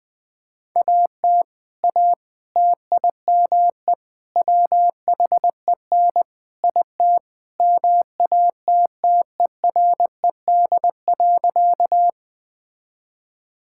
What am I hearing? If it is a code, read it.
Morse code, heard as a keyed tone, about 19 wpm, 700 Hz. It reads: AT A TIME WHEN IT MATTERED.
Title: Sentences with 4-Word Phrases - 20wpm